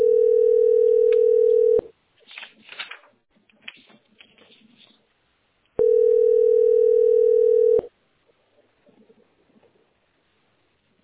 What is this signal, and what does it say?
Ringback tone (North America), 440 + 480 Hz.